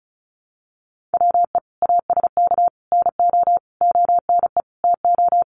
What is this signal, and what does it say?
Morse code, heard as a keyed tone, about 34 wpm, 700 Hz. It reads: WE ASK NO ONE TO
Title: Sentences from Top 1500 Words 35wpm